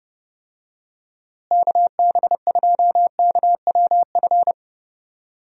Morse code, heard as a keyed tone, about 30 wpm, 700 Hz. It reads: KB2KWF
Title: USA Call Signs - 6 Characters - Repeated 2x - 30wpm